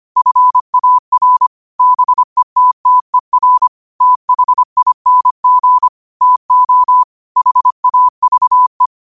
Morse code, heard as a keyed tone, about 25 wpm, 1000 Hz. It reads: FAR BETTER THING TO HAVE
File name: 25wpm